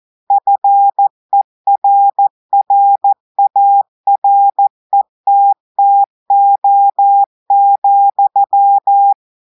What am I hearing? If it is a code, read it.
Morse code, heard as a keyed tone, about 14 wpm, 800 Hz. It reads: FERRARETTO,
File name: audios / portugais / portugais texte en clair 1 vitesse 14 mots minutes